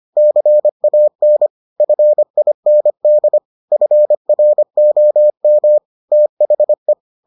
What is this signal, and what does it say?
Morse code, heard as a keyed tone, about 25 wpm, 600 Hz. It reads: CAN FIND FROM THE